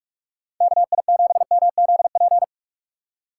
Morse code, heard as a keyed tone, about 59 wpm, 700 Hz. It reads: KI7MZP